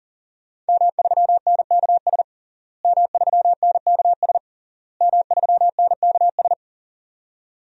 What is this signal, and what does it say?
Morse code, heard as a keyed tone, about 40 wpm, 700 Hz. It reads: M3NKS M3NKS M3NKS
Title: DXCC-100 Call Signs - 5 Characters - Repeated 3x - 40wpm